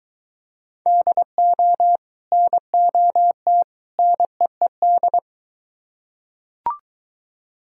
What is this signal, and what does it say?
Morse code, heard as a keyed tone, about 23 wpm, 700 Hz. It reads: DO NOT NEED